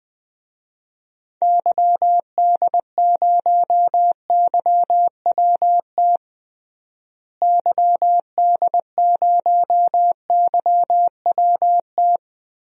Morse code, heard as a keyed tone, about 21 wpm, 700 Hz. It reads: YD0YWT YD0YWT
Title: DXCC-100 Call Signs - 6 Characters - Repeated 3x - 20wpm